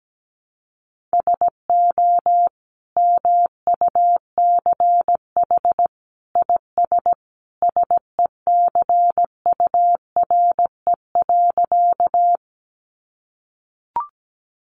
Morse code, heard as a keyed tone, about 17 wpm, 700 Hz. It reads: SO MUCH IS SECURE.
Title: Sentences from Top 1200 Words 17wpm